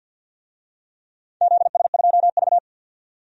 Morse code, heard as a keyed tone, about 45 wpm, 700 Hz. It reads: ZS2V
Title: DXCC-100 Call Signs - 4 Characters - 50wpm